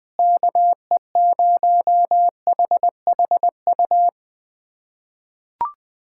Morse code, heard as a keyed tone, about 20 wpm, 700 Hz. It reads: KE0HHU
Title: USA Callsign DE Callsign 20wpm